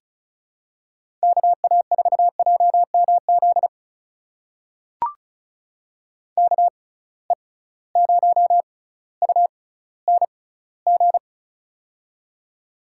Morse code, heard as a keyed tone, about 36 wpm, spaced as for 18 wpm, 700 Hz. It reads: KA4JMZ KE0UNG
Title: USA Call Signs - 6 Characters - 6x Character Spacing - 35wpm